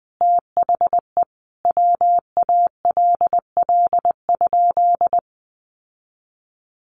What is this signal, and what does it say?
Morse code, heard as a keyed tone, about 20 wpm, 700 Hz. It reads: THE WALL?
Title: Sentences from Top 1100 Words 20wpm